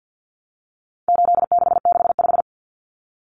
Morse code, heard as a keyed tone, about 50 wpm, 700 Hz. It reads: 8665
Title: Numbers - 4-Digit 50wpm